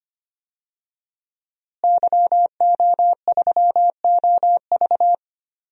Morse code, heard as a keyed tone, about 25 wpm, 700 Hz. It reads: YO3OV
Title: DXCC-100 Call Signs - 5 Characters - Repeated 3x - 25wpm